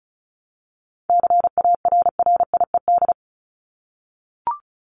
Morse code, heard as a keyed tone, about 34 wpm, 700 Hz. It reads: CARRIED
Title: ED Suffix - Top 100 Words with Base Words 35wpm